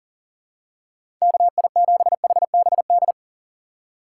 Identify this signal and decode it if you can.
Morse code, heard as a keyed tone, about 40 wpm, 700 Hz. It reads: KI7HBD
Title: USA Call Signs - 6 Characters - 3x Character Spacing - 40wpm